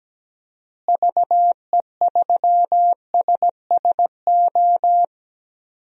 Morse code, heard as a keyed tone, about 17 wpm, 700 Hz. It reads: VE3SSO
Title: DXCC-100 Call Signs - 6 Characters - Repeated 3x - 17wpm